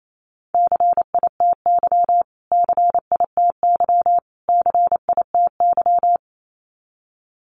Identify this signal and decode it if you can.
Morse code, heard as a keyed tone, about 28 wpm, 700 Hz. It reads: CITY CITY CITY
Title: Top 500 Words Repeated 28wpm